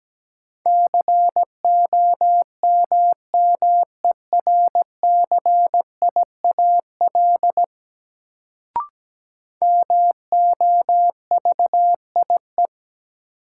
Morse code, heard as a keyed tone, about 17 wpm, 700 Hz. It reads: COMMERCIAL MOVIE
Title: Sentences from Top 1000 Words - Review Part 5 - New Word Drill - 17wpm